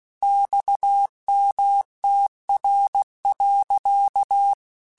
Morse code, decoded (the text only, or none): XMTR.